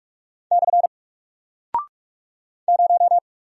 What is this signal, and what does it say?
Morse code, heard as a keyed tone, about 46 wpm, 700 Hz. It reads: / 0
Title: Single Character (Letter:Number:Punctuation:Prosign) Alt 45wpm